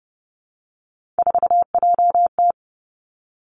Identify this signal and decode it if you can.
Morse code, heard as a keyed tone, about 30 wpm, 700 Hz. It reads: 4JT
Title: Three Challenging Letter-Number Combinations 30wpm